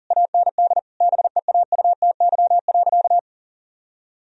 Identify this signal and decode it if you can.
Morse code, heard as a keyed tone, about 40 wpm, 700 Hz. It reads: AND BEAUTY.
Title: Sentences from Top 1000 Words - Review Part 2 - 40wpm